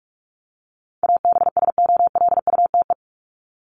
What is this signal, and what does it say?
Morse code, heard as a keyed tone, about 44 wpm, 700 Hz. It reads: ABSOLUTE
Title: LY Suffix - Top 100 Words with Base Words 45wpm